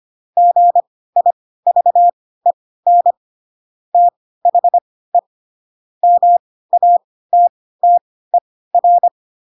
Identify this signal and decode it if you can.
Morse code, heard as a keyed tone, about 25 wpm, spaced as for 16 wpm, 700 Hz. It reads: GIVEN THE MATTER
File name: Pinocchio / Pinocchio ch3 10 25 ik1hge